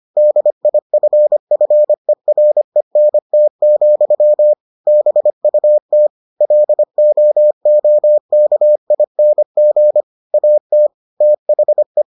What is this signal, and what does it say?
Morse code, heard as a keyed tone, about 25 wpm, 600 Hz. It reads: DIFFERENT, BUT LOOKING AT THE